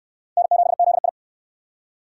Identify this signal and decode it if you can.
Morse code, heard as a keyed tone, about 68 wpm, 700 Hz. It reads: N7ZI